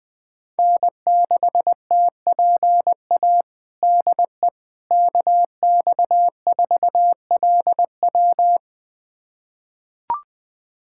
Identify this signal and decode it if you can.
Morse code, heard as a keyed tone, about 20 wpm, 700 Hz. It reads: N6TPA DE KX4LW
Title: USA Callsign DE Callsign - 5 Characters - 20wpm